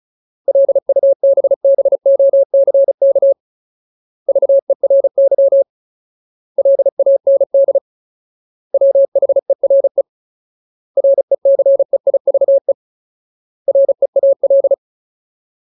Morse code, heard as a keyed tone, about 34 wpm, 550 Hz. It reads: LUBBOCK VERY LAND WHERE RECEIVE REAL
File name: Ultimate Word List / 35WPM Ultimate